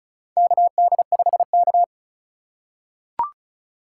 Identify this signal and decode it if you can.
Morse code, heard as a keyed tone, about 36 wpm, 700 Hz. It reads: KD5K
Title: USA Call Signs - 2x Character Spacing - 35wpm